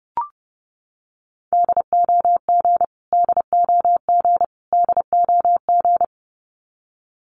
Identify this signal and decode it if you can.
Morse code, heard as a keyed tone, about 30 wpm, 700 Hz. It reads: DOG DOG DOG
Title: Top 500 Words Repeated 30wpm